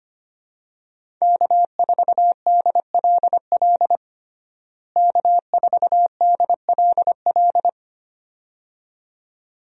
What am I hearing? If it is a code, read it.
Morse code, heard as a keyed tone, about 25 wpm, 700 Hz. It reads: K4DLL K4DLL
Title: USA Call Signs - 5 Characters - Repeated 2x - 25wpm